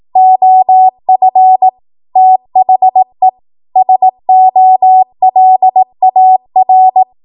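Morse code, heard as a keyed tone, about 18 wpm, 750 Hz. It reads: OF THE SOLAR
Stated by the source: ARRL 18 wpm Morse code practice archive transmission.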